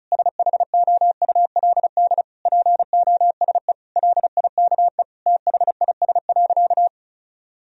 Morse code, heard as a keyed tone, about 34 wpm, 700 Hz. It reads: SHOULD POSE LIKE THIS.
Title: Sentences from Top 1000 Words - Review Part 2 - 35wpm